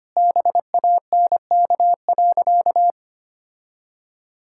Morse code, heard as a keyed tone, about 25 wpm, 700 Hz. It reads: BANK.
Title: Sentences from Top 1000 Words - Review Part 5 - 25wpm